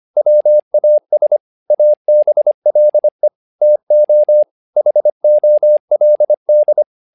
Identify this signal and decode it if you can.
Morse code, heard as a keyed tone, about 25 wpm, 600 Hz. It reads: WAS ABLE TO HOLD
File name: learning center / tower trouble / tower trouble 25wpm